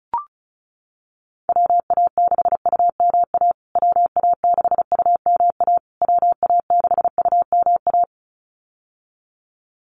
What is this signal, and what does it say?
Morse code, heard as a keyed tone, about 34 wpm, 700 Hz. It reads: WA6UMA WA6UMA WA6UMA
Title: USA Call Signs Repeated 3x 35wpm